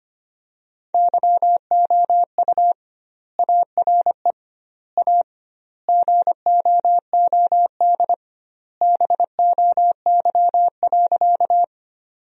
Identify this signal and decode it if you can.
Morse code, heard as a keyed tone, about 25 wpm, 700 Hz. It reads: YOU ARE A GOOD BOY.
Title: 2x Word-Spacing - Sentences from Top 200 Words - 25wpm